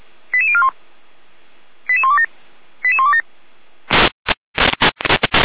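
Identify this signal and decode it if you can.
Five-tone selective call (ZVEI), heard as quick runs of five tones, tones of about 70 ms. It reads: 88041 88127 88127
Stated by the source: ZVEI five-tone selective call 88199 received by SDR, archived by feuerwehr-orlamuende fire brigade server.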